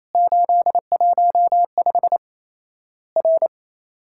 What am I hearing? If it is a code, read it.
Morse code, heard as a keyed tone, about 28 wpm, 700 Hz. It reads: 815 R
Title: Worked All Europe DX Contest - QTC Exchange - 28wpm